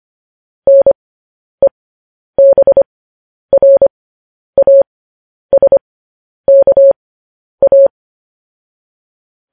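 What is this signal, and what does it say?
Morse code, heard as a keyed tone, about 25 wpm, spaced as for 10 wpm, 550 Hz. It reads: NEBRASKA
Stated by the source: Morse code sending of news headlines, speed 10.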